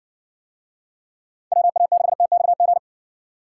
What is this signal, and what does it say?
Morse code, heard as a keyed tone, about 58 wpm, 700 Hz. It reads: WA6TXG